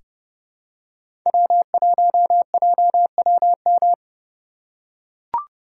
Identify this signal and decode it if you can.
Morse code, heard as a keyed tone, about 30 wpm, 700 Hz. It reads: W1JWM
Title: USA Call Signs - 5 Characters - 3x Character Spacing - 30wpm